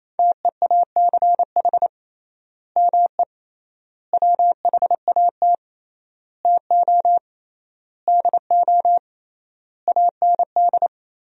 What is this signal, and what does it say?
Morse code, heard as a keyed tone, about 28 wpm, 700 Hz. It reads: TEACH ME WHAT TO DO AND
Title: 3x Word-Spacing - Sentences from Top 500 Words - 28wpm